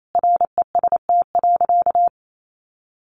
Morse code, heard as a keyed tone, about 28 wpm, 700 Hz. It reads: REST.
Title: Sentences from Top 1500 Words 28wpm